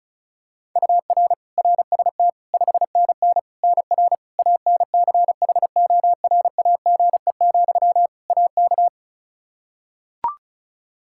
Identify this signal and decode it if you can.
Morse code, heard as a keyed tone, about 34 wpm, 700 Hz. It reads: UR RST 5NN NR ANCHORAGE, AK
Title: Practice QSO - Exchange 1 - Beginning - 35wpm